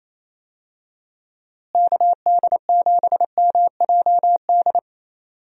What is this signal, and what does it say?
Morse code, heard as a keyed tone, about 28 wpm, 700 Hz. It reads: KD7MJD
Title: USA Call Signs - 6 Characters - Repeated 3x - 28wpm